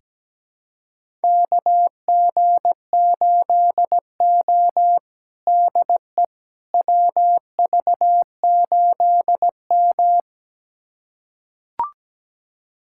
Morse code, heard as a keyed tone, about 17 wpm, 700 Hz. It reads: KG8O DE WV8M
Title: USA Callsign DE Callsign - 4 Characters - 17wpm